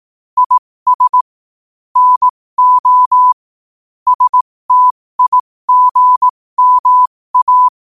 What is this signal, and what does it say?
Morse code, heard as a keyed tone, about 18 wpm, spaced as for 15 wpm, 1000 Hz. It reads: IS NO STIGMA
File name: qotdfiles / 15wpm